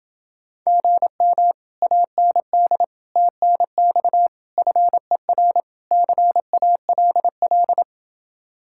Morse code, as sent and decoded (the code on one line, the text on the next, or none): --. -- / .- -. -.. / - -. -..- / ..-. .-. / -.-. .- .-.. .-..
GM AND TNX FER CALL